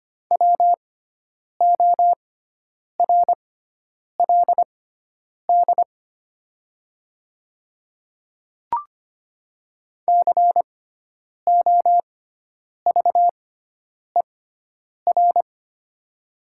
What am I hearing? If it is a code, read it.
Morse code, heard as a keyed tone, about 25 wpm, spaced as for 9 wpm, 700 Hz. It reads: WORLD COVER
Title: 6x Character Spacing - Top 500 Words - 25wpm